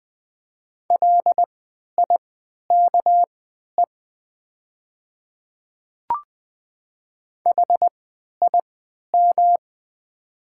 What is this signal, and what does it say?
Morse code, heard as a keyed tone, about 20 wpm, spaced as for 11 wpm, 690 Hz. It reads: LIKE HIM
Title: 3x Character Spacing - Top 100 Words - 20wpm